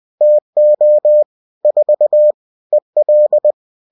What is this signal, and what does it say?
Morse code, heard as a keyed tone, about 20 wpm, 600 Hz. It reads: TO 4 EL